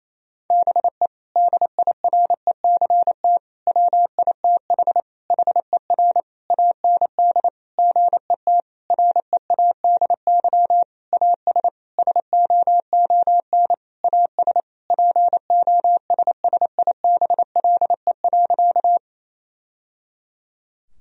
Morse code, heard as a keyed tone, about 28 wpm, 700 Hz. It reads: BE DIRECT WITH HER AND GET READY AS SOON AS POSSIBLE.
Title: Sentences from Top 1000 Words - Review Part 2 - 28wpm